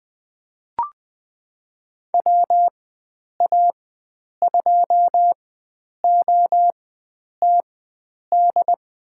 Morse code, heard as a keyed tone, about 20 wpm, spaced as for 9 wpm, 700 Hz. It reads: WA2OTD
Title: USA Call Signs - 4x Character Spacing - 20wpm